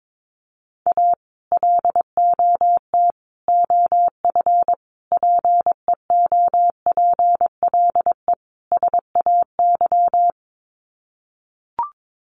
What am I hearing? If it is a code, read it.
Morse code, heard as a keyed tone, about 22 wpm, 700 Hz. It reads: A LOT OF PEOPLE SAY
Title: Sets of 5 Words Encore2 22wpm